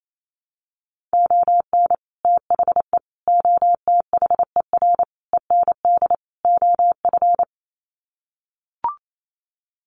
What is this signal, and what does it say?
Morse code, heard as a keyed tone, about 28 wpm, 700 Hz. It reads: ON THE OTHER END OF
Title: Sets of 5 Words Encore 28wpm